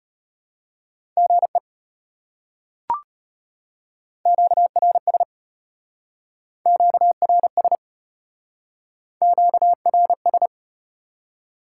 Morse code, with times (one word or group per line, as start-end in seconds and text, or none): 1.2-1.6 GE
4.3-5.2 QRS
6.7-7.8 QRS
9.2-10.5 QRS